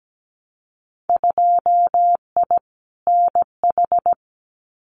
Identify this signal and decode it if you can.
Morse code, heard as a keyed tone, about 17 wpm, 700 Hz. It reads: 2I NH
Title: Winter Field Day Exchanges Encore 17wpm